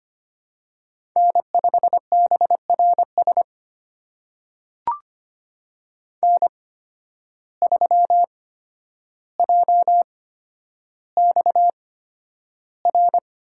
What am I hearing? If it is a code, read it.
Morse code, heard as a keyed tone, about 25 wpm, spaced as for 11 wpm, 700 Hz. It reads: N5BRS N3JXR
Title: USA Call Signs - 5 Characters - 8x Character Spacing - 25wpm